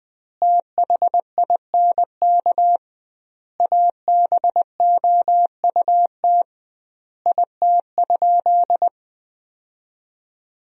Morse code, heard as a keyed tone, about 20 wpm, 700 Hz. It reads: THINK ABOUT IT?
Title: 2x Word-Spacing - Sentences from Top 200 Words - 20wpm